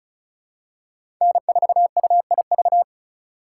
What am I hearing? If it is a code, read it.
Morse code, heard as a keyed tone, about 36 wpm, 700 Hz. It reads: N4UIV
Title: USA Call Signs - 5 Characters - 3x Character Spacing - 35wpm